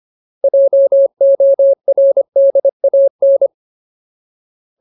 Morse code, heard as a keyed tone, about 25 wpm, 550 Hz. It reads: JORDAN
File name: Ultimate Word List / 25WPM Ultimate